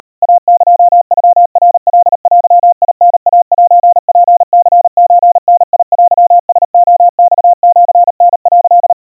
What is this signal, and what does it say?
Morse code (Japanese Wabun code), heard as a keyed tone, about 38 wpm, 700 Hz. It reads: イエノナカデタイセツニソダテラレマシタ」
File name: kawa / Morse sample / kamyu 140 cpm 700 Hz Crispy